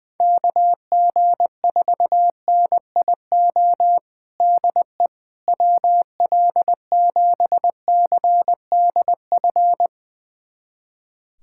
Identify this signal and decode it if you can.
Morse code, heard as a keyed tone, about 21 wpm, 700 Hz. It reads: KG4NIO DE WL7CDF
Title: USA Callsign DE Callsign - 6 Characters - 20wpm